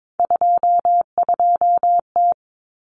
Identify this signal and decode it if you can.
Morse code, heard as a keyed tone, about 22 wpm, 700 Hz. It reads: 22T
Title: Common Cut Numbers - 3-Digit 22wpm